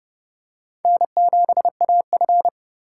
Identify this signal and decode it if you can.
Morse code, heard as a keyed tone, about 30 wpm, 700 Hz. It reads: N7AF
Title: USA Call Signs - 4 Characters - 6x Character Spacing - 30wpm